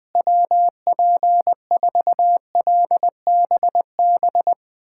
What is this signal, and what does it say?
Morse code, heard as a keyed tone, about 20 wpm, 700 Hz. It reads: WP4LBB
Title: USA Callsign DE Callsign - 6 Characters - 20wpm